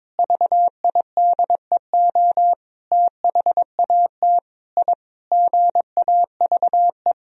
Morse code, heard as a keyed tone, about 22 wpm, 700 Hz. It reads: VIDEO THAT I GAVE
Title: Sentences from Top 1000 Words - Review Part 4 - 22wpm